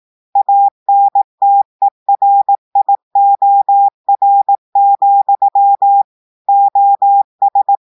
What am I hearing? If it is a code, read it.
Morse code, heard as a keyed tone, about 18 wpm, 800 Hz. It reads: ANTERIOR, OS
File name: audios / portugais / portugais texte en clair 1 vitesse 18 mots minutes